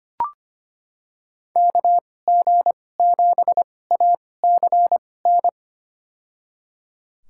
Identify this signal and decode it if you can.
Morse code, heard as a keyed tone, about 25 wpm, spaced as for 18 wpm, 700 Hz. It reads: KG7ACN
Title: USA Call Signs - 6 Characters - 2x Character Spacing - 25wpm